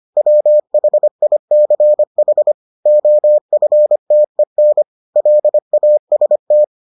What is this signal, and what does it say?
Morse code, heard as a keyed tone, about 25 wpm, 600 Hz. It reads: WHICH OFTEN LAST